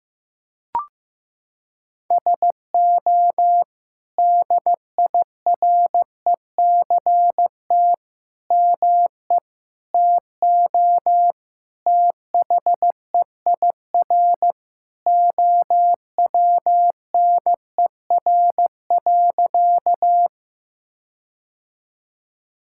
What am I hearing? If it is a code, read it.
Morse code, heard as a keyed tone, about 15 wpm, 700 Hz. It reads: SO DIRECT ME TO THEIR OWNER.
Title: Sentences from Top 1600 Words 15wpm